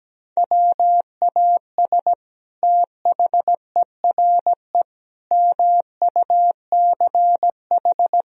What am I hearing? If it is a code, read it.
Morse code, heard as a keyed tone, about 17 wpm, 700 Hz. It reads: WAS THERE MUCH
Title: Sentences from Top 1000 Words - Review Part 4 - 17wpm